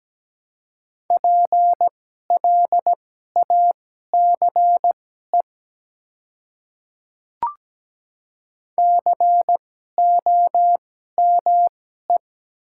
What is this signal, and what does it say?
Morse code, heard as a keyed tone, about 17 wpm, spaced as for 12 wpm, 700 Hz. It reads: PLACE COME
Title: 2x Character Spacing - Top 200 Words - 17wpm